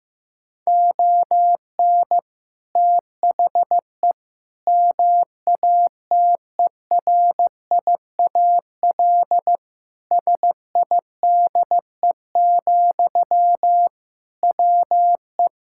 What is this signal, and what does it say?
Morse code, heard as a keyed tone, about 16 wpm, 700 Hz. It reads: ON THE MATERIAL SIDE, WE
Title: Sentences from Top 1000 Words - Review Part 5 - 15wpm